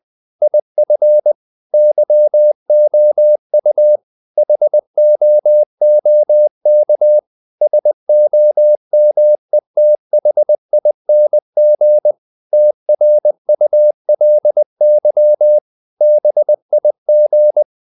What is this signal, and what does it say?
Morse code, heard as a keyed tone, about 20 wpm, 600 Hz. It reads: IF YOU HOOK SOMETHING TRULY BIG
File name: the old man and the sea / the old man and the sea part 1 20wpm